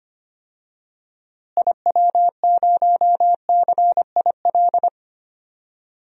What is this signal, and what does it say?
Morse code, heard as a keyed tone, about 25 wpm, 700 Hz. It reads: IW0CIL
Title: DXCC-100 Call Signs - 6 Characters - Repeated 2x - 25wpm